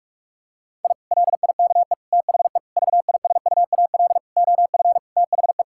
Morse code, heard as a keyed tone, about 42 wpm, 700 Hz. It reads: I LIKE THE VISUAL OF THE
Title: Sentences from Top 1000 Words - Review Part 6 - 45wpm